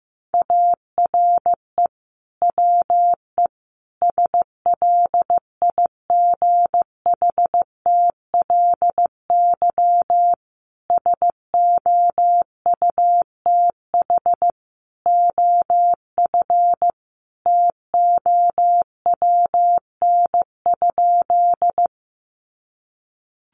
Morse code, heard as a keyed tone, about 15 wpm, 700 Hz. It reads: ARE WE SLIGHTLY SOUTH OF TOWN?
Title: Sentences from Top 1300 Words 15wpm